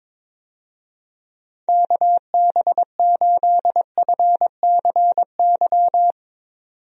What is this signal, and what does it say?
Morse code, heard as a keyed tone, about 21 wpm, 700 Hz. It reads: KB8FCY
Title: USA Call Signs - 6 Characters - 22wpm